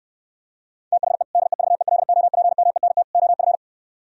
Morse code, heard as a keyed tone, about 67 wpm, 700 Hz. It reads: THE DEVELOPMENT OF